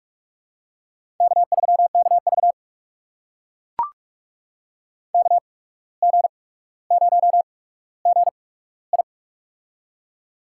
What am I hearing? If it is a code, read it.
Morse code, heard as a keyed tone, about 44 wpm, spaced as for 18 wpm, 700 Hz. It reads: K3KV KG0GI
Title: USA Call Signs - 8x Character Spacing - 45wpm